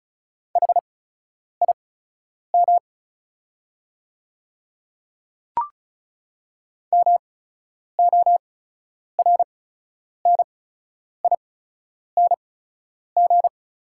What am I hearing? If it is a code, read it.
Morse code, heard as a keyed tone, about 36 wpm, spaced as for 10 wpm, 690 Hz. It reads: HIM MORNING
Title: 8x Character Spacing - Top 500 Words - 35wpm